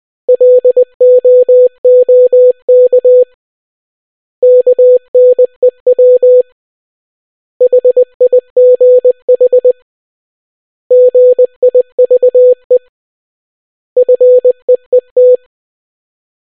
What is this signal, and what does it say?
Morse code, heard as a keyed tone, about 20 wpm, 500 Hz. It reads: LOOK KNEW HIGH GIVE FEET